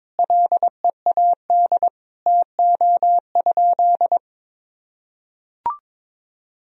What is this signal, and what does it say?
Morse code, heard as a keyed tone, about 22 wpm, 700 Hz. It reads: LEAD TO?
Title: Sentences from Top 1000 Words - Review Part 2 - 22wpm